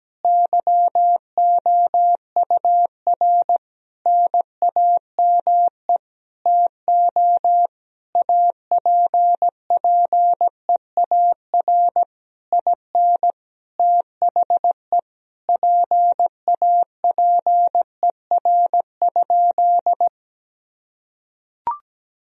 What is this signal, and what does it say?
Morse code, heard as a keyed tone, about 17 wpm, 700 Hz. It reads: YOUR NAME TO APPEAR IN THE PAPER?
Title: Sentences from Top 1000 Words - Review Part 2 - 17wpm